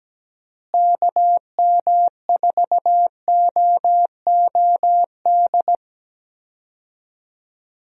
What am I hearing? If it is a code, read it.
Morse code, heard as a keyed tone, about 17 wpm, 700 Hz. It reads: KM4OOD
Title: USA Call Signs - 6 Characters - Repeated 3x - 17wpm